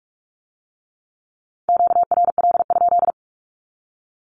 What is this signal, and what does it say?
Morse code, heard as a keyed tone, about 46 wpm, 700 Hz. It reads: QRL?
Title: Q-Codes with Spoken Meaning - 45wpm